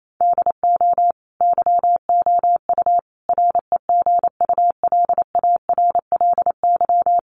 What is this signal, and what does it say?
Morse code, heard as a keyed tone, about 28 wpm, 700 Hz. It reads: DO YOU REGULARLY
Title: Sentences from Top 1400 Words 28wpm